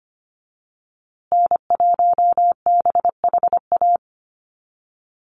Morse code, heard as a keyed tone, about 25 wpm, 700 Hz. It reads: N1BHA
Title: USA Call Signs Repeated 3x 25wpm